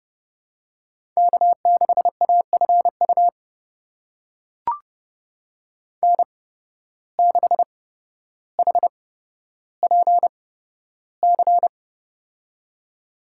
Morse code, heard as a keyed tone, about 31 wpm, spaced as for 13 wpm, 700 Hz. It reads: K6AFU N6HPC
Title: USA Call Signs - 8x Character Spacing - 30wpm